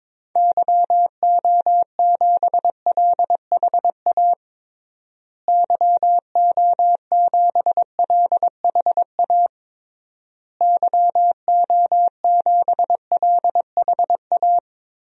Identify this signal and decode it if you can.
Morse code, heard as a keyed tone, about 22 wpm, 700 Hz. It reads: YO7LHA YO7LHA YO7LHA
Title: DXCC-100 Call Signs - 6 Characters - Repeated 3x - 22wpm